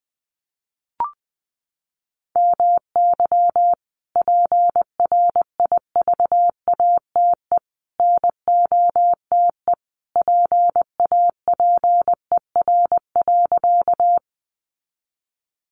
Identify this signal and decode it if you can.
Morse code, heard as a keyed tone, about 20 wpm, 700 Hz. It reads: MY PRIVATE NOTE PAPER.
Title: Sentences from Top 1000 Words 20wpm